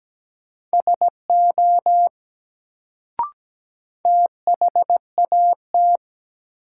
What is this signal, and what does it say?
Morse code, heard as a keyed tone, about 17 wpm, 700 Hz. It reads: SO THAT
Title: Sentences from Top 100 Words - New Word Drill - 17wpm